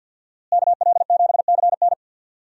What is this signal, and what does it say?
Morse code, heard as a keyed tone, about 52 wpm, 700 Hz. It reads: KL7CN
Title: USA Call Signs - 5 Characters - Repeated 2x - 50wpm